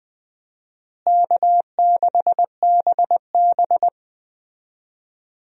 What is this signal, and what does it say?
Morse code, heard as a keyed tone, about 20 wpm, 700 Hz. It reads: K6BB
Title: USA Call Signs - 4 Characters - Repeated 3x - 20wpm